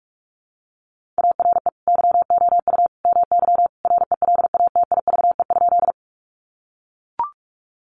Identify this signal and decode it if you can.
Morse code, heard as a keyed tone, about 44 wpm, 700 Hz. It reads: ARE YOU MY RELATIVE?
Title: Sentences from Top 1400 Words 45wpm